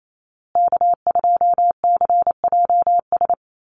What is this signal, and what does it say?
Morse code, heard as a keyed tone, about 28 wpm, 700 Hz. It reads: K2CJS
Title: USA Call Signs Repeated 3x 28wpm